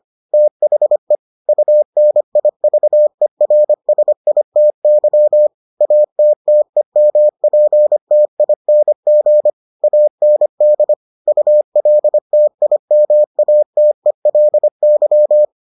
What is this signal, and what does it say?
Morse code, heard as a keyed tone, about 25 wpm, 600 Hz. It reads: THE UNIVERSITY ATTEMPTING AND ULTIMATELY